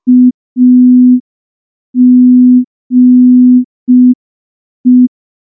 Morse code, heard as a keyed tone, about 5 wpm, 250 Hz. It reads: AGE
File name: morse / morse-training-05WPM